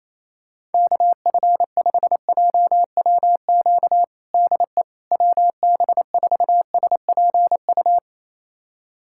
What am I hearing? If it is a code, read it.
Morse code, heard as a keyed tone, about 28 wpm, 700 Hz. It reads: KF5JWQ DE WB4SPU
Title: USA Callsign DE Callsign - 6 Characters - 28wpm